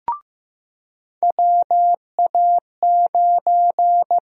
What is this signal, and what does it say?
Morse code, heard as a keyed tone, about 15 wpm, 700 Hz. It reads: WA9
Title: USA Call Sign Prefixes - Repeated 2x - 15wpm